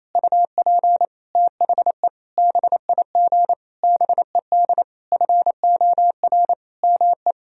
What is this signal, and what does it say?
Morse code, heard as a keyed tone, about 27 wpm, 700 Hz. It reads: UP THE BIG BED FOR ME
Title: Sentences from Top 1000 Words - Review Part 1 - 28wpm